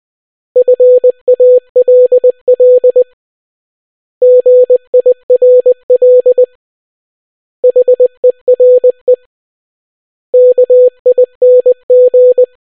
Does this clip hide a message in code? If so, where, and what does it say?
Yes: Morse code from 0.6 to 12.4 s, heard as a keyed tone, about 20 wpm, 500 Hz. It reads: FALL GIRL HERE KING